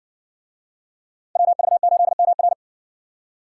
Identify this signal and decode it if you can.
Morse code, heard as a keyed tone, about 64 wpm, 700 Hz. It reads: WV8MR